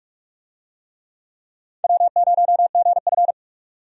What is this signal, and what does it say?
Morse code, heard as a keyed tone, about 46 wpm, 700 Hz. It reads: W0GF
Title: USA Call Signs - 4 Characters - 45wpm